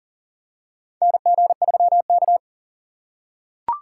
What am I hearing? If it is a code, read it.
Morse code, heard as a keyed tone, about 40 wpm, 700 Hz. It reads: NG3K
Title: USA Call Signs - 4 Characters - 2x Character Spacing - 40wpm